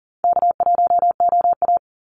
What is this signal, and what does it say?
Morse code, heard as a keyed tone, about 40 wpm, 700 Hz. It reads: K1OA
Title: Call Signs Repeated 2x - 40wpm